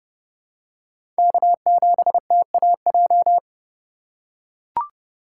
Morse code, heard as a keyed tone, about 30 wpm, 700 Hz. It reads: K7TAJ
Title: USA Call Signs - 4x Character Spacing - 30wpm